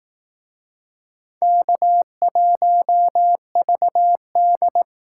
Morse code, heard as a keyed tone, about 19 wpm, 700 Hz. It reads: K1VD